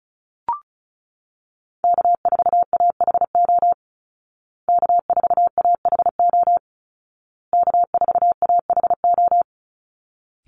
Morse code, heard as a keyed tone, about 34 wpm, 700 Hz. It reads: K4AHO K4AHO K4AHO
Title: Call Signs Repeated 2x - 35wpm